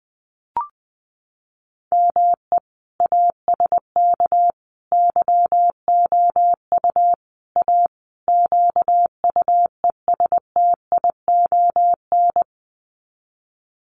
Morse code, heard as a keyed tone, about 20 wpm, 700 Hz. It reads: ME ASK YOU A QUESTION
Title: Sets of 5 Words Encore2 20wpm